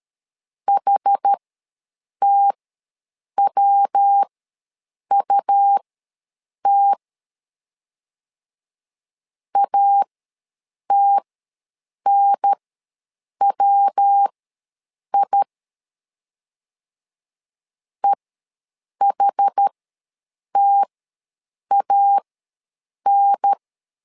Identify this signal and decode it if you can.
Morse code, heard as a keyed tone, about 13 wpm, spaced as for 7 wpm, 800 Hz. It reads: HTWUT ATNWI EHTAN
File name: LEZIONI / 3 MWRO / 3 MW ws 12wpm 1816